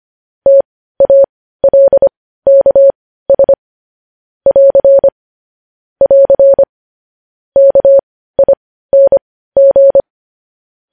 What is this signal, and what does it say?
Morse code, heard as a keyed tone, about 25 wpm, spaced as for 15 wpm, 550 Hz. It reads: TALKS <AR> <AR> KING
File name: podcast / News headlines-1x-25x15